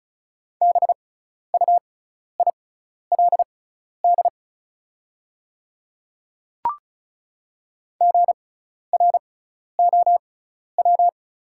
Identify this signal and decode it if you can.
Morse code, heard as a keyed tone, about 35 wpm, spaced as for 12 wpm, 690 Hz. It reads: BUILD GROW
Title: 6x Character Spacing - Top 300 Words - 35wpm